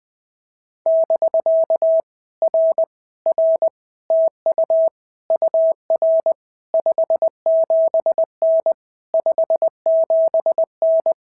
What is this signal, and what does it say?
Morse code, heard as a keyed tone, about 19 wpm, 650 Hz. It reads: <BK> R R TU UR 57N 57N
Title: POTA QSO Practice - 20wpm